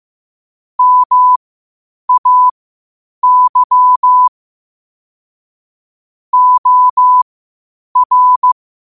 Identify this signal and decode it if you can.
Morse code, heard as a keyed tone, about 15 wpm, spaced as for 8 wpm, 1000 Hz. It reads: MAY OR
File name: qotdfiles / 8wpm